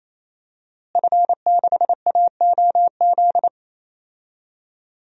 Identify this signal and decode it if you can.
Morse code, heard as a keyed tone, about 28 wpm, 700 Hz. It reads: F6AOZ
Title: DXCC-100 Call Signs - 5 Characters - Repeated 3x - 28wpm